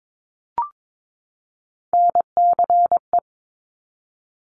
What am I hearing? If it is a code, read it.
Morse code, heard as a keyed tone, about 22 wpm, 700 Hz. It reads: NCE
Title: Top Three Letter Combinations in Words 22wpm